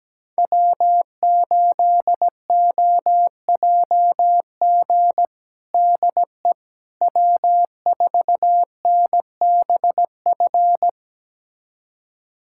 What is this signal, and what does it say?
Morse code, heard as a keyed tone, about 17 wpm, 700 Hz. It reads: W8OJG DE W4NBF
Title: USA Callsign DE Callsign - 5 Characters - 17wpm